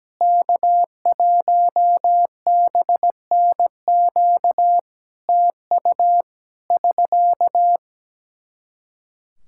Morse code, heard as a keyed tone, about 17 wpm, 700 Hz. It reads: K1BNQ TU <SK>
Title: Practice QSO - Exchange 4 - Ending - 17wpm